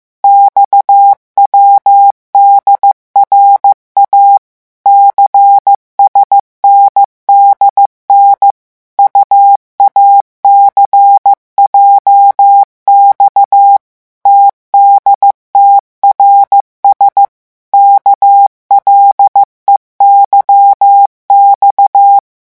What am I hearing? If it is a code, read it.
Morse code, heard as a keyed tone, about 15 wpm, 800 Hz. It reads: XWDRA CSNDN UACJX TDTRS KLEYX
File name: morse mp3 / revision7-vitesse 14 mots